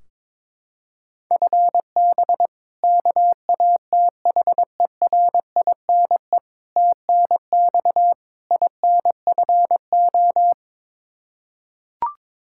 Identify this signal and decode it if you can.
Morse code, heard as a keyed tone, about 22 wpm, 700 Hz. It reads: FB KATHERINE TNX INFO
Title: K1USN SST QSO Practice 22wpm